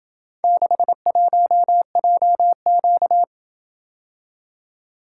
Morse code, heard as a keyed tone, about 26 wpm, 700 Hz. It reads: 61JQ